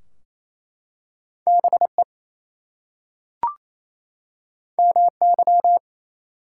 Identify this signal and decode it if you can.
Morse code, heard as a keyed tone, about 27 wpm, 700 Hz. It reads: BE MY
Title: Two Letter Words - 28wpm